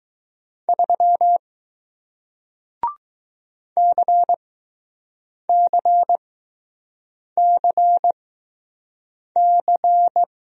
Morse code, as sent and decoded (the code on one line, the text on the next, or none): ...-- / -.-. -.-. -.-. -.-.
3 CCCC